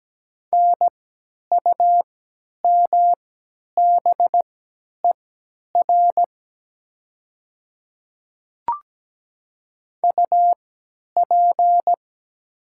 Morse code, heard as a keyed tone, about 17 wpm, spaced as for 10 wpm, 700 Hz. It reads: NUMBER UP
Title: 3x Character Spacing - Top 100 Words - 17wpm